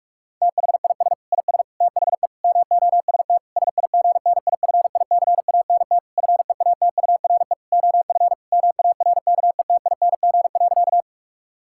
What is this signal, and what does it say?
Morse code, heard as a keyed tone, about 46 wpm, 700 Hz. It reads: THIS IS THE MOST SIGNIFICANT FEATURE OF MARKETING.
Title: Sentences from Top 1000 Words - Review Part 6 - 45wpm